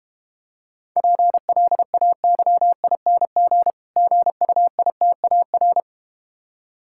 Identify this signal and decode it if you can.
Morse code, heard as a keyed tone, about 31 wpm, 700 Hz. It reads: PLAYING GUITAR